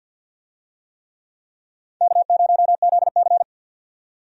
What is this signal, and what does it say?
Morse code, heard as a keyed tone, about 52 wpm, 700 Hz. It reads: K0ZC
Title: USA Call Signs - 4 Characters - 50wpm